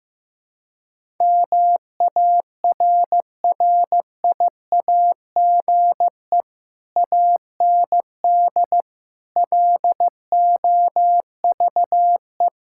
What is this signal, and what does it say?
Morse code, heard as a keyed tone, about 15 wpm, 700 Hz. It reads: MARRIAGE AND LOVE
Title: Sentences from Top 1000 Words - Review Part 5 - 15wpm